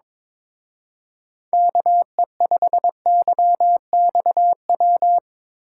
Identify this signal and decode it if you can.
Morse code, heard as a keyed tone, about 24 wpm, 700 Hz. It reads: KE5YXW
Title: Speed Racing - USA 2x3 Call Signs 22wpm